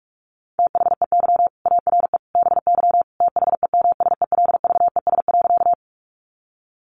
Morse code, heard as a keyed tone, about 44 wpm, 700 Hz. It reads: THEY ARE BY THEMSELVES.
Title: Sentences from Top 800 Words 45wpm